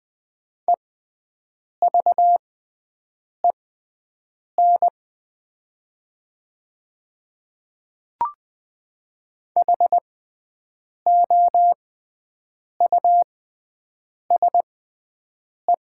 Morse code, heard as a keyed tone, about 20 wpm, spaced as for 7 wpm, 700 Hz. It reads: EVEN HOUSE
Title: 6x Character Spacing - Top 300 Words - 20wpm